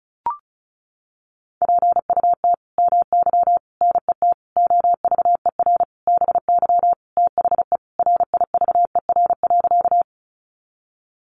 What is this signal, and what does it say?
Morse code, heard as a keyed tone, about 34 wpm, 700 Hz. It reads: PUT MY NET OVER BY THE RIVER.
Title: Sentences from Top 1400 Words 35wpm